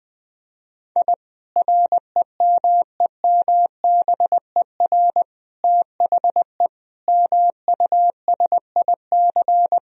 Morse code, heard as a keyed tone, about 21 wpm, 700 Hz. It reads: I REMEMBER THE MUSIC
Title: Sentences from Top 1000 Words - Review Part 1 - 20wpm